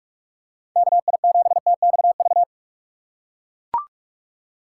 Morse code, heard as a keyed tone, about 43 wpm, 700 Hz. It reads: KI7TXV
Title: USA Call Signs - 6 Characters - 4x Character Spacing - 45wpm